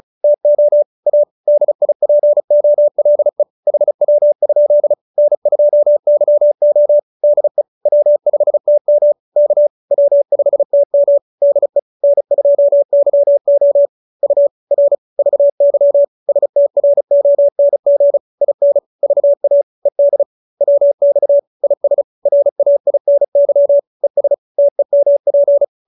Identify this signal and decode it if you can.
Morse code, heard as a keyed tone, about 34 wpm, 570 Hz. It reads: TO A DIPOLE HW? N2YO DE W5TM K W5TM DE N2YO U R VY STRONG IN VA ED WX IS RAINY ES TEMP